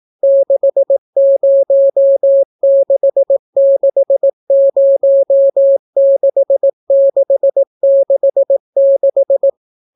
Morse code, heard as a keyed tone, about 18 wpm, 550 Hz. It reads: 606606666